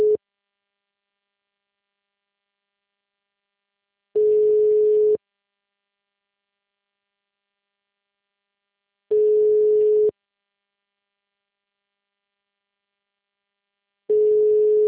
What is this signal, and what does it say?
Ringback tone (Europe, 425 Hz).